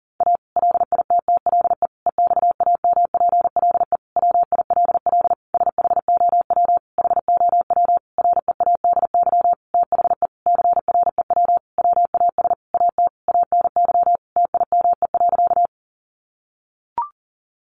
Morse code, heard as a keyed tone, about 37 wpm, 700 Hz. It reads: A LITTLE EXAMPLE WILL SHOW HOW READY THE CREW WAS AT ANY TIME.
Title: Sentences from Top 1500 Words 40wpm